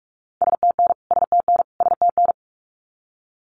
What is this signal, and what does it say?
Morse code, heard as a keyed tone, about 44 wpm, 700 Hz. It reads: STN STN STN
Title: Abbreviations Repeated 3x - 45wpm